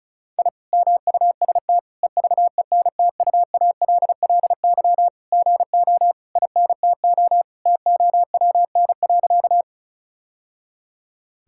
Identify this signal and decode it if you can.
Morse code, heard as a keyed tone, about 34 wpm, 700 Hz. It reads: I MUST EVENTUALLY GO INTO TOWN.
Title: Sentences from Top 1000 Words - Review Part 6 - 35wpm